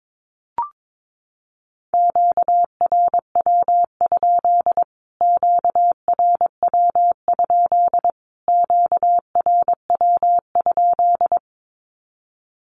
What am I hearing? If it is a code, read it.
Morse code, heard as a keyed tone, about 22 wpm, 700 Hz. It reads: QRW? QRW? QRW?
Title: Q-Codes Repeated 3x 22wpm